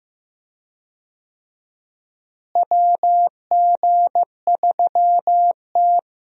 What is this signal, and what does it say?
Morse code, heard as a keyed tone, about 16 wpm, 700 Hz. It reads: WG3T
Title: USA Call Signs - 4 Characters - Repeated 3x - 15wpm